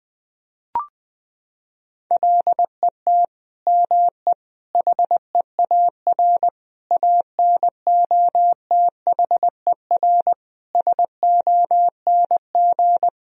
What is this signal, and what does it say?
Morse code, heard as a keyed tone, about 20 wpm, 700 Hz. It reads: LET ME HEAR ANOTHER SONG
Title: Sentences from Top 1000 Words - Review Part 2 - 20wpm